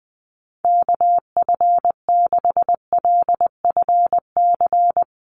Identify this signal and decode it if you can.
Morse code, heard as a keyed tone, about 20 wpm, 700 Hz. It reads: KF6LFC
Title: USA Call Signs Repeated 3x 20wpm